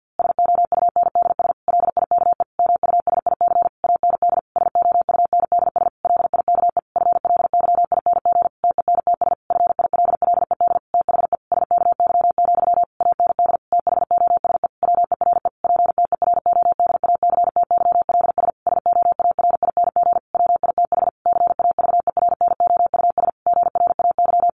Morse code, heard as a keyed tone, about 50 wpm, 700 Hz. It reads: SOUNDS LIKE MUSIC AND SOUNDS LIKE FLYING TENTS FILLED THE SKY, AND THOSE WERE PTERODACTYLS SOARING WITH CAVERNOUS GRAY
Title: A Sound Of Thunder 50wpm